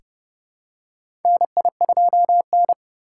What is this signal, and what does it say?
Morse code, heard as a keyed tone, about 30 wpm, 700 Hz. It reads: NI2N
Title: USA Call Signs - 2x Character Spacing - 30wpm